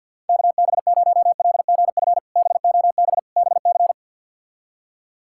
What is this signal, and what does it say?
Morse code, heard as a keyed tone, about 48 wpm, 700 Hz. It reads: KB0LGF BOB BC
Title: K1USN SST QSO Practice - 50wpm